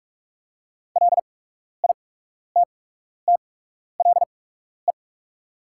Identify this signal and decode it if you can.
Morse code, heard as a keyed tone, about 47 wpm, spaced as for 12 wpm, 690 Hz. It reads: LITTLE
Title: 8x Character Spacing - Top 200 Words - 45wpm